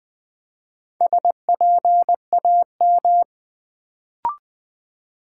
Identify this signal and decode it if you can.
Morse code, heard as a keyed tone, about 20 wpm, 700 Hz. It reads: SPAM
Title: Build A Word - 2 Letters At A Time - 4-Letter Words - 20wpm